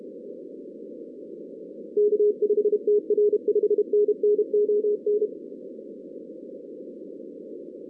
Morse code, heard as a keyed tone, about 31 wpm, 420 Hz. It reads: K5TR 5NN ON
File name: 2016 arrl dx cw / 2016-ARRL-DX-CW-K5TR- - 0-46